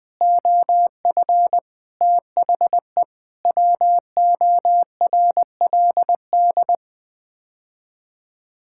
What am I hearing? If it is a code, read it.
Morse code, heard as a keyed tone, about 20 wpm, 700 Hz. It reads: OF THE WORLD
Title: Sets of 6 Words - 20wpm